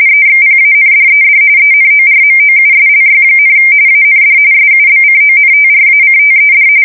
RTTY, about 45.45 baud. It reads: EXPANDED ENORMOUSLY, AND SUBCONTRACTED
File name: fsk170 / usnavy4